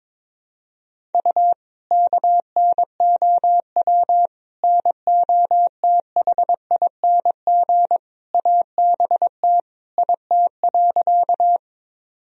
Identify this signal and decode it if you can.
Morse code, heard as a keyed tone, about 22 wpm, 700 Hz. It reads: U KNOW NOTHING ABT IT.
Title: Sentences from Top 500 Words and Abbreviations 22wpm